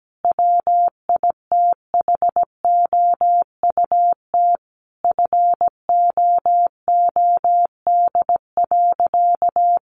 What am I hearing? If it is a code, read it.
Morse code, heard as a keyed tone, about 17 wpm, 700 Hz. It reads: WITHOUT FOOD.
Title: Sentences from Top 1500 Words 17wpm